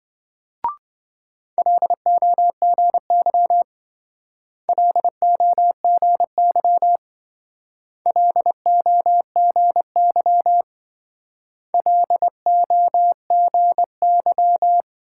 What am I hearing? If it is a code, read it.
Morse code, heard as a keyed tone, about 26 wpm, 700 Hz. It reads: LOGY LOGY LOGY LOGY